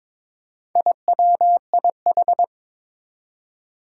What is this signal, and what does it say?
Morse code, heard as a keyed tone, about 22 wpm, 700 Hz. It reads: IWIH
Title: Four Challenging Letter-Number Combinations 22wpm